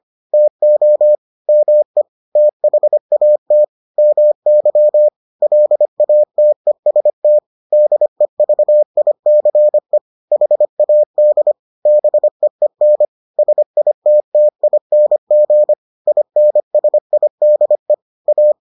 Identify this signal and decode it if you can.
Morse code, heard as a keyed tone, about 25 wpm, 600 Hz. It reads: TO ME THAT MY LATEST DEVICE HAD BEEN SITTING INSIDE A